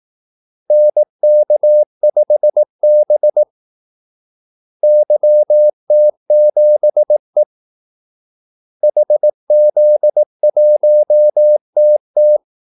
Morse code, heard as a keyed tone, about 18 wpm, 600 Hz. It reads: NK5B YT7E HZ1TT
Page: Kaldesignaler 18 wpm